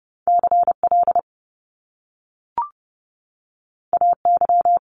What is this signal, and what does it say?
Morse code, heard as a keyed tone, about 30 wpm, 700 Hz. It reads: CL AY
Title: Top Two Letter Combinations in Words 30wpm